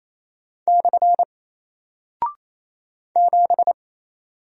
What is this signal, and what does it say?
Morse code, heard as a keyed tone, about 27 wpm, 700 Hz. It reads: / 7
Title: Single Character (Letter:Number:Punctuation:Prosign) Alt 28wpm